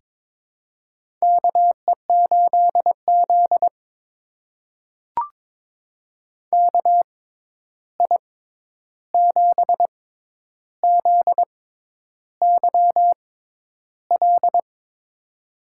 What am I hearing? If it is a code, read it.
Morse code, heard as a keyed tone, about 22 wpm, spaced as for 10 wpm, 700 Hz. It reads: KE8Z KI7ZYL
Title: USA Call Signs - 6x Character Spacing - 22wpm